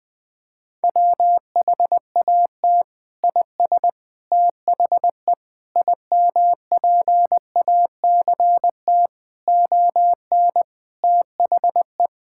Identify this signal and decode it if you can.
Morse code, heard as a keyed tone, about 20 wpm, 700 Hz. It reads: WHAT IS THE IMPACT ON THE
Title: Sentences from Top 1000 Words - Review Part 6 - 20wpm